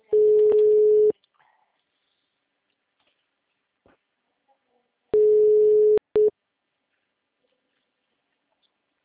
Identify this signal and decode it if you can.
Ringback tone (Europe, 425 Hz).